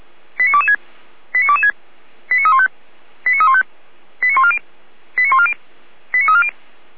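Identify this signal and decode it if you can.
Five-tone selective call (ZVEI), heard as quick runs of five tones, tones of about 70 ms. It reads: 89226 89226 89315 89315 89140 89140 89340
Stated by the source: ZVEI five-tone selective call received with an SDR, from the audio server of Feuerwehr Orlamünde.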